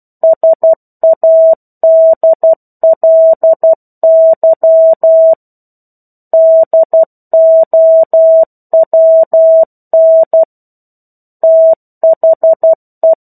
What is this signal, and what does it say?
Morse code, heard as a keyed tone, about 12 wpm, 650 Hz. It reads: SADLY DOWN THE